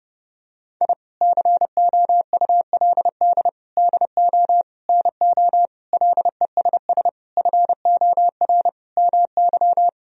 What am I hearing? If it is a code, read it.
Morse code, heard as a keyed tone, about 30 wpm, 700 Hz. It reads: I COULD DO NO LESS FOR MY
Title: Sentences from Top 1000 Words - Review Part 2 - 30wpm